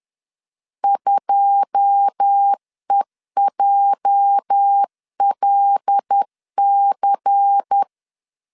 Morse code, heard as a keyed tone, about 10 wpm, 800 Hz. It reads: 2EJLC
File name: LEZIONI / 9 2867 10wpm 2733